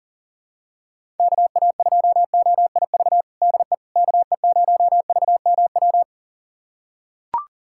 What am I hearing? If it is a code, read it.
Morse code, heard as a keyed tone, about 37 wpm, 700 Hz. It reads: KA2OIV DE KE0VMW
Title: USA Callsign DE Callsign - 6 Characters - 40wpm